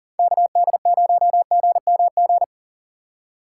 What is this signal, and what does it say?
Morse code, heard as a keyed tone, about 39 wpm, 700 Hz. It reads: KD0GMG
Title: USA Call Signs - 6 Characters - 3x Character Spacing - 40wpm